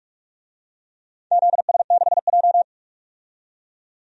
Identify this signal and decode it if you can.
Morse code, heard as a keyed tone, about 46 wpm, 700 Hz. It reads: ZS6J
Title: DXCC-100 Call Signs - 4 Characters - 45wpm